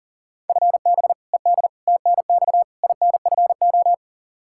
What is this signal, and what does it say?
Morse code, heard as a keyed tone, about 40 wpm, 700 Hz. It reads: FB ED TNX INFO
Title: K1USN SST QSO Practice 40wpm